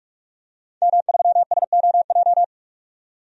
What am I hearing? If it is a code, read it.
Morse code, heard as a keyed tone, about 42 wpm, 700 Hz. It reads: M3SOJ
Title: DXCC-100 Call Signs - 5 Characters - 45wpm